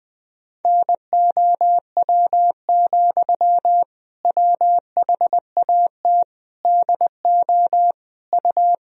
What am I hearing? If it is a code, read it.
Morse code, heard as a keyed tone, about 21 wpm, 700 Hz. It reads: NOW, WHAT DO U
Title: Sentences from Top 200 Words and Abbreviations 20wpm